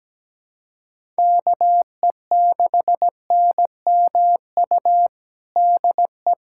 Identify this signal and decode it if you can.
Morse code, heard as a keyed tone, about 17 wpm, 700 Hz. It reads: KE6NMU DE
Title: USA Callsign DE Callsign 17wpm